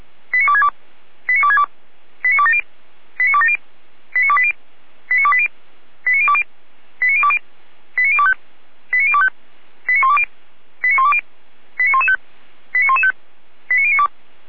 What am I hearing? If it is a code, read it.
Five-tone selective call (ZVEI), heard as quick runs of five tones, tones of about 70 ms. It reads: 89262 89262 89270 89270 89280 89280 89020 89020 89025 89025 89120 89120 89115 89115 89092